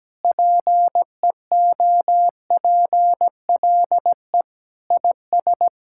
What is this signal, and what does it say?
Morse code, heard as a keyed tone, about 17 wpm, 700 Hz. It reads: PEOPLE IS
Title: Sentences from Top 1000 Words - Review Part 6 - 17wpm